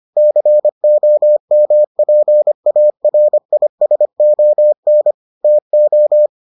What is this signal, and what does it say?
Morse code, heard as a keyed tone, about 25 wpm, 600 Hz. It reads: COMPARISON TO